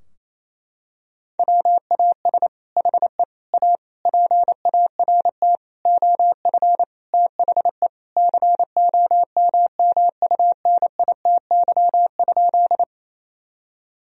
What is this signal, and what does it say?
Morse code, heard as a keyed tone, about 28 wpm, 700 Hz. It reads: WAS HE A PART OF THE COMMUNITY?
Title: Sentences from Top 1000 Words - Review Part 4 - 28wpm